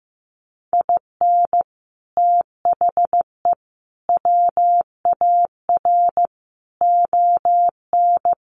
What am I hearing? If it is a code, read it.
Morse code, heard as a keyed tone, about 15 wpm, 700 Hz. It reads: IN THE WAR ON
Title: Sets of 5 Words Encore2 15wpm